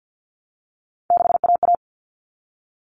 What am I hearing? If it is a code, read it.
Morse code, heard as a keyed tone, about 50 wpm, 700 Hz. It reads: 6AA
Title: All Cut Numbers - 3-Digit 50wpm